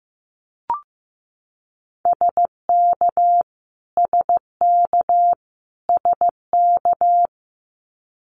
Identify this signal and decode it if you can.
Morse code, heard as a keyed tone, about 15 wpm, 700 Hz. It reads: SK SK SK
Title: Canadian Provinces and Territories - Repeated 3x - 15wpm